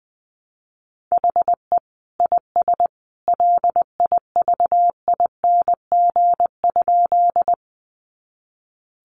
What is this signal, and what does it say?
Morse code, heard as a keyed tone, about 20 wpm, 700 Hz. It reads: HE IS LIVING?
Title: Sentences from Top 1100 Words 20wpm